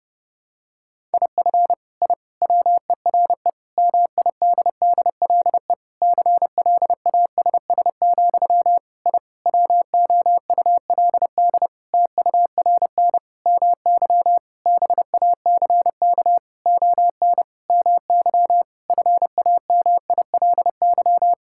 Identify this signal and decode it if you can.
Morse code, heard as a keyed tone, about 32 wpm, 700 Hz. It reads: IF I WERE MIDDLE CLASS, I WOULD TURN MY BACK ON MY FAMILY
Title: Sentences from Top 1000 Words - Review Part 6 - 30wpm